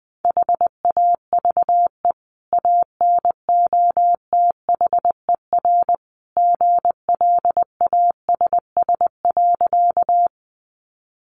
Morse code, heard as a keyed tone, about 20 wpm, 700 Hz. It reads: HAVE ANOTHER GLASS.
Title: Sentences from Top 1300 Words 20wpm